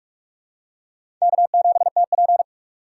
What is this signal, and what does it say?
Morse code, heard as a keyed tone, about 46 wpm, 690 Hz. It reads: K7TP
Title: USA Call Signs - 4 Characters - 2x Character Spacing - 45wpm